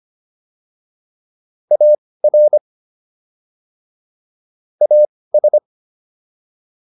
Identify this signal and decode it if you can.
Morse code, heard as a keyed tone, about 25 wpm, spaced as for 18 wpm, 600 Hz. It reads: AR AS